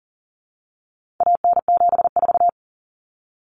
Morse code, heard as a keyed tone, about 40 wpm, 700 Hz. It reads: AN74
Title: Common Cut Numbers - 4-Digit 40wpm